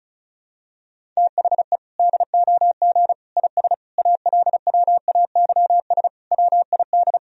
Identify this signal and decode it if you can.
Morse code, heard as a keyed tone, about 36 wpm, 700 Hz. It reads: THE DOG IS ALWAYS WID
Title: Sentences from Top 400 Words and Abbreviations 35wpm